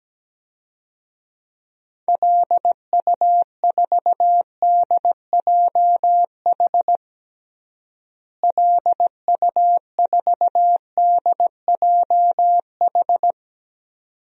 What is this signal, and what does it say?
Morse code, heard as a keyed tone, about 17 wpm, 700 Hz. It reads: LU4DJH LU4DJH
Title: DXCC-100 Call Signs - 6 Characters - Repeated 2x - 17wpm